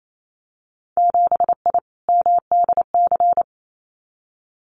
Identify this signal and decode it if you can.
Morse code, heard as a keyed tone, about 28 wpm, 700 Hz. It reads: 7I MDC
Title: Winter Field Day Exchanges Encore 28wpm